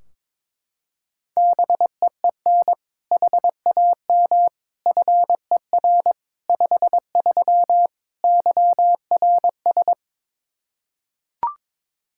Morse code, heard as a keyed tone, about 22 wpm, 700 Hz. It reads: BEEN HAM FER 53 YRS
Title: Practice QSO - Exchange 3 - More Info - 22wpm